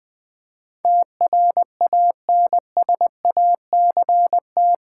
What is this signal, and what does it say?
Morse code, heard as a keyed tone, about 20 wpm, 700 Hz. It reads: TRANSACT
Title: Build A Word - 4 Letters At A Time - 11-Letter Words - 20wpm